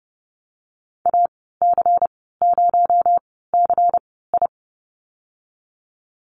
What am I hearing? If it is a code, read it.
Morse code, heard as a keyed tone, about 30 wpm, spaced as for 17 wpm, 700 Hz. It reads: AC0CI
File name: Lesson 37g - Letter x - Call Signs 30wpm